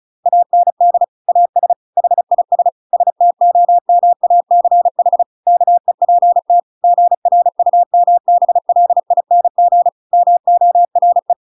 Morse code, heard as a keyed tone, about 34 wpm, 700 Hz. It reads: AND AS HIS STOMACH KEPT GRUMBLING MORE